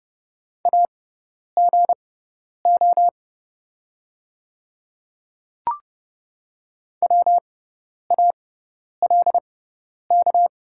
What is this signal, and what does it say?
Morse code, heard as a keyed tone, about 30 wpm, spaced as for 10 wpm, 690 Hz. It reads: AGO WALK
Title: 6x Character Spacing - Top 500 Words - 30wpm